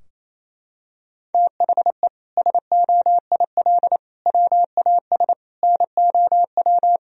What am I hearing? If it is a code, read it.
Morse code, heard as a keyed tone, about 28 wpm, 700 Hz. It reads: THE SOIL WAS NOW
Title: Sentences from Top 1000 Words - Review Part 4 - 28wpm